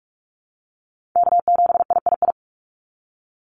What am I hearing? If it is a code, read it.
Morse code, heard as a keyed tone, about 44 wpm, 700 Hz. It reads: K7III
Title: NA SOTA Chaser Callsigns 45wpm